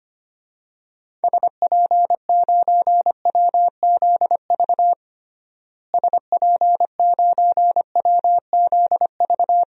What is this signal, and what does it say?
Morse code, heard as a keyed tone, about 25 wpm, 700 Hz. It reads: SP9WZV SP9WZV
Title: DXCC-100 Call Signs - 6 Characters - Repeated 2x - 25wpm